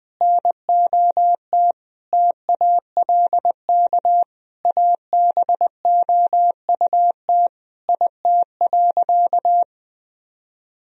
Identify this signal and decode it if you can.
Morse code, heard as a keyed tone, about 20 wpm, 700 Hz. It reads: NOT TALK ABOUT IT.
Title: Sentences from Top 1000 Words - Review Part 4 - 20wpm